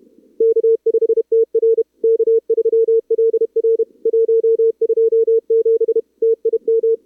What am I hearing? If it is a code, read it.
Morse code, heard as a keyed tone, about 31 wpm, 440 Hz. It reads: K5TR K3LR 127 TIM
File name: CWSprint / Feb2022 / k5tr-feb-na-sprint-cw-k5ot -0000